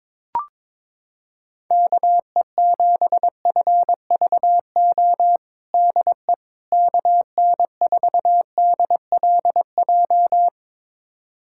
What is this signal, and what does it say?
Morse code, heard as a keyed tone, about 22 wpm, 700 Hz. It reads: KE7FVO DE KN4DLJ
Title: USA Callsign DE Callsign - 6 Characters - 22wpm